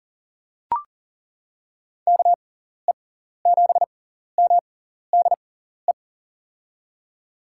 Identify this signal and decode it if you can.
Morse code, heard as a keyed tone, about 40 wpm, spaced as for 14 wpm, 700 Hz. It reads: KE7MDE
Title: USA Call Signs - 6 Characters - 6x Character Spacing - 40wpm